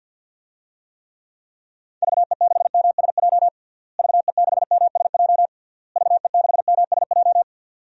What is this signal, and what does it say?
Morse code, heard as a keyed tone, about 53 wpm, 700 Hz. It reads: VE6MSJ VE6MSJ VE6MSJ
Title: DXCC-100 Call Signs - 6 Characters - Repeated 3x - 50wpm